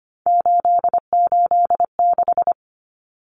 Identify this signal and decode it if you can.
Morse code, heard as a keyed tone, about 25 wpm, 700 Hz. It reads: 886
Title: Numbers - 3-Digit 25wpm